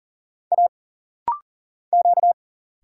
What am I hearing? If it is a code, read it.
Morse code, heard as a keyed tone, about 39 wpm, 700 Hz. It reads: A Q
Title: Single Letter-Number - Rapid-Fire 40wpm